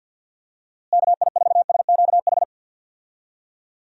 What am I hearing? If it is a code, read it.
Morse code, heard as a keyed tone, about 52 wpm, 700 Hz. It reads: KI4SQH
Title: USA Call Signs - 6 Characters - Repeated 3x - 50wpm